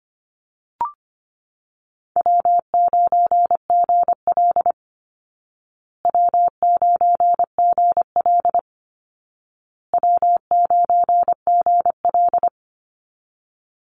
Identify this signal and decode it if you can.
Morse code, heard as a keyed tone, about 25 wpm, 700 Hz. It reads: W9GL W9GL W9GL
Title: Easy Call Signs Repeated 3x - 25wpm